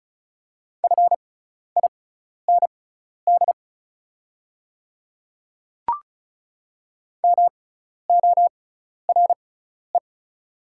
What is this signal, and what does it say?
Morse code, heard as a keyed tone, about 36 wpm, spaced as for 12 wpm, 700 Hz. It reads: FIND MORE
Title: 6x Character Spacing - Top 100 Words - 35wpm